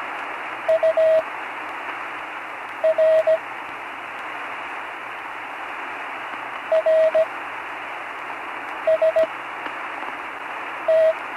Morse code, heard as a keyed tone, about 17 wpm, spaced as for 5 wpm, 640 Hz, with radio noise. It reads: UR RST
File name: CW Learning Programs / Learning Morse Code (from Tune in the World) / 43 on the air